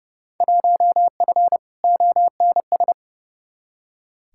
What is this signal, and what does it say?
Morse code, heard as a keyed tone, about 30 wpm, 700 Hz. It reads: 1F ONS
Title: Field Day Exchanges Encore - 30wpm